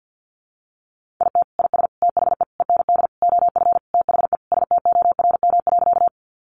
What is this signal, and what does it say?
Morse code, heard as a keyed tone, about 50 wpm, 700 Hz. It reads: IT IS THE END OF THE STORM.
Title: Sentences from Top 1500 Words 50wpm